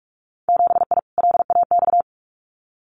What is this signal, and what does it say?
Morse code, heard as a keyed tone, about 46 wpm, 700 Hz. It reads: 7I LAX
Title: Winter Field Day Exchanges Encore 45wpm